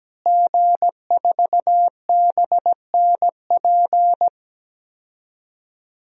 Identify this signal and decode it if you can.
Morse code, heard as a keyed tone, about 17 wpm, 700 Hz. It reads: G4BNP
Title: DXCC-100 Call Signs - 5 Characters - Repeated 3x - 17wpm